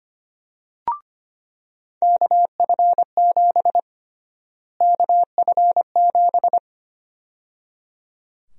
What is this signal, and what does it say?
Morse code, heard as a keyed tone, about 25 wpm, 700 Hz. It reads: KF7 KF7
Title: USA Call Sign Prefixes - Repeated 2x - 25wpm